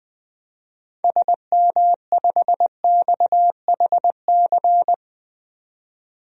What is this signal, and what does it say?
Morse code, heard as a keyed tone, about 20 wpm, 700 Hz. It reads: SM5XHC
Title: DXCC-100 Call Signs - 6 Characters - 20wpm